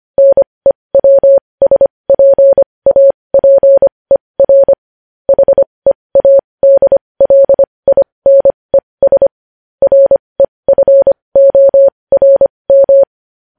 Morse code, heard as a keyed tone, about 25 wpm, spaced as for 20 wpm, 550 Hz. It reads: NEWSPAPER HEADLINES REFORM
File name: podcast / News headlines-1x-25x20